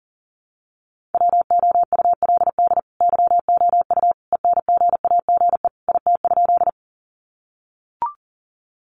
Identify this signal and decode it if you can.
Morse code, heard as a keyed tone, about 40 wpm, 700 Hz. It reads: WOULD YOU ENGAGE IT?
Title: Sentences from Top 1500 Words 40wpm